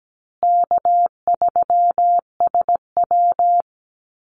Morse code, heard as a keyed tone, about 17 wpm, 700 Hz. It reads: K3SW
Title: Easy Call Signs Repeated 3x - 17wpm